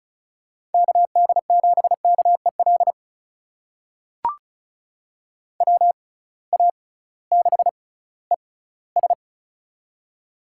Morse code, heard as a keyed tone, about 36 wpm, spaced as for 19 wpm, 700 Hz. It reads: KD7KEL WA6ES
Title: USA Call Signs - 6x Character Spacing - 35wpm